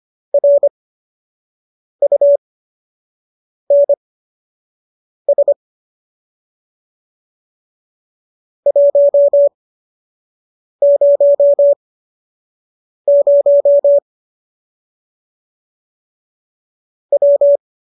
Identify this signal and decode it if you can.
Morse code, heard as a keyed tone, about 25 wpm, spaced as for 6 wpm, 570 Hz. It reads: RUNS 100 W